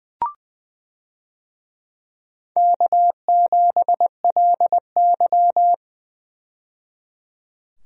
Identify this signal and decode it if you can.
Morse code, heard as a keyed tone, about 20 wpm, 700 Hz. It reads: K7LY
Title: USA Call Signs - 4 Characters - 20wpm